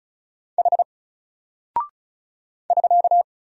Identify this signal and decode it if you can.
Morse code, heard as a keyed tone, about 33 wpm, 700 Hz. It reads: H <SK>
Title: Single Character (Letter:Number:Punctuation:Prosign) 35wpm